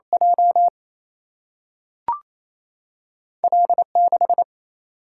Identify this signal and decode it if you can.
Morse code, heard as a keyed tone, about 28 wpm, 700 Hz. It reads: J L6
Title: Single-Letter Single-Number 28wpm